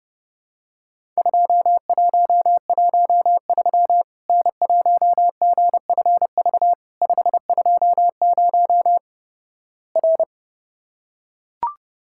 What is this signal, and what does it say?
Morse code, heard as a keyed tone, about 30 wpm, 700 Hz. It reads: 2113 N1GFV 520 R
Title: Worked All Europe DX Contest - QTC Exchange - 30wpm